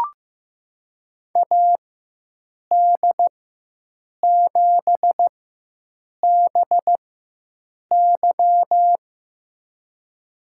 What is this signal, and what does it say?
Morse code, heard as a keyed tone, about 16 wpm, spaced as for 7 wpm, 700 Hz. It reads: AD7BY
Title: USA Call Signs - 5 Characters - 4x Character Spacing - 15wpm